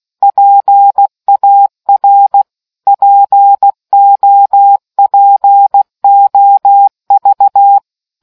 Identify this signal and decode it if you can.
Morse code, heard as a keyed tone, about 16 wpm, 800 Hz. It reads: PAR POPOV